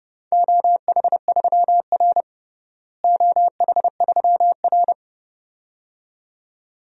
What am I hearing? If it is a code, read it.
Morse code, heard as a keyed tone, about 30 wpm, 700 Hz. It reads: OH3R OH3R
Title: DXCC-100 Call Signs - 4 Characters - Repeated 2x - 30wpm